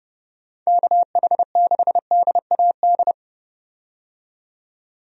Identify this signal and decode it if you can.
Morse code, heard as a keyed tone, about 30 wpm, 700 Hz. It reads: KH6DAD
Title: DXCC-100 Call Signs - 6 Characters - 30wpm